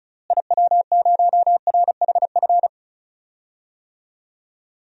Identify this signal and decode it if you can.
Morse code, heard as a keyed tone, about 36 wpm, 700 Hz. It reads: IW0RHF
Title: DXCC-100 Call Signs - 6 Characters - Repeated 3x - 35wpm